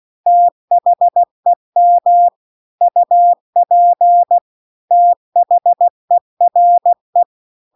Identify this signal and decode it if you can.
Morse code, heard as a keyed tone, about 16 wpm, 700 Hz. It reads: THEM UP THERE